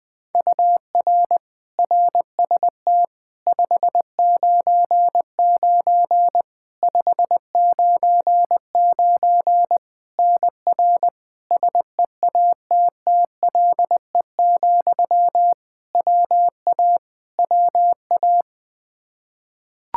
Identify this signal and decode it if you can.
Morse code, heard as a keyed tone, about 21 wpm, 700 Hz. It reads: UR RST 599 599 NR SEATTLE, WA WA
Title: Practice QSO - Exchange 1 - Beginning - 20wpm